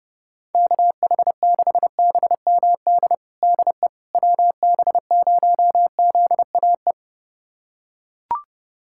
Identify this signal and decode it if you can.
Morse code, heard as a keyed tone, about 30 wpm, 690 Hz. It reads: KH6BMD DE WB0ZAE
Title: USA Callsign DE Callsign - 6 Characters - 30wpm